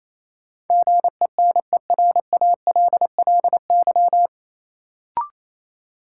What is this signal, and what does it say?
Morse code, heard as a keyed tone, about 28 wpm, 700 Hz. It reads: GENERALLY
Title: Sentences from Top 600 Words - New Word Drill - 28wpm